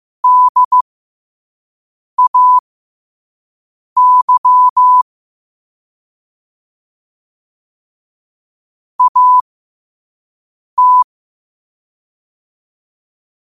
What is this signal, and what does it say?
Morse code, heard as a keyed tone, about 15 wpm, spaced as for 5 wpm, 1000 Hz. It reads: DAY AT
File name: qotdfiles / 5wpm